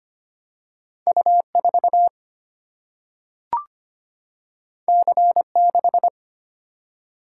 Morse code, heard as a keyed tone, about 25 wpm, 700 Hz. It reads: U 4 C 6
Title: Single-Letter Single-Number 25wpm